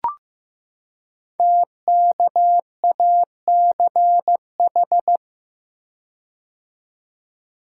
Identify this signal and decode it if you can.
Morse code, heard as a keyed tone, about 15 wpm, 700 Hz. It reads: TKACH
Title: Code Groups - 5 Letters - 15wpm